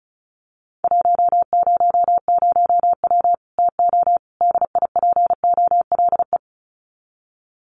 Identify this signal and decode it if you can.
Morse code, heard as a keyed tone, about 34 wpm, 700 Hz. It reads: 100W TO DIPOLE
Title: Common QSO Phrases 35wpm